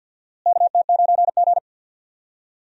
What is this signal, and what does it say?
Morse code, heard as a keyed tone, about 49 wpm, 700 Hz. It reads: KT9G
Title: USA Call Signs - 4 Characters - Repeated 2x - 50wpm